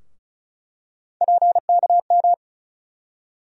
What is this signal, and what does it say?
Morse code, heard as a keyed tone, about 36 wpm, 700 Hz. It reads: PKM
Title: USA Call Sign Postfixes - 3 Letters 35wpm